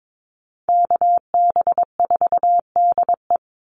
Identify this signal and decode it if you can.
Morse code, heard as a keyed tone, about 22 wpm, 700 Hz. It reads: KB4DE
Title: Call Signs - 22wpm